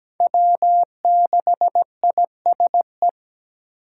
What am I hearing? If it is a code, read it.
Morse code, heard as a keyed tone, about 17 wpm, 700 Hz. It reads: W6ISE
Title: USA Call Signs - 5 Characters - 3x Character Spacing - 17wpm